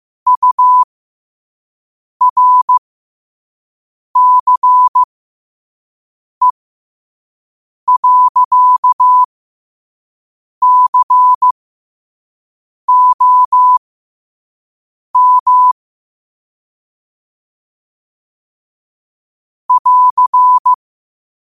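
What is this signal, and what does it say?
Morse code, heard as a keyed tone, about 15 wpm, spaced as for 5 wpm, 1000 Hz. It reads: URCE.COM <AR>
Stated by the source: Morse code practice at 5 wpm.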